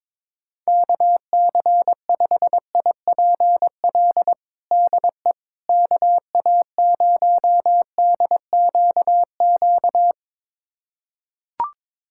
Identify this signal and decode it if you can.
Morse code, heard as a keyed tone, about 22 wpm, 700 Hz. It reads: KC5IPL DE KA0DQQ
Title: USA Callsign DE Callsign - 6 Characters - 22wpm